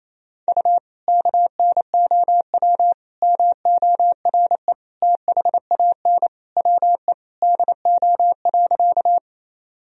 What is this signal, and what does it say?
Morse code, heard as a keyed tone, about 28 wpm, 700 Hz. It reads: U KNOW MORE THAN WE DO.
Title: Sentences from Top 100 Words and Abbreviations 28wpm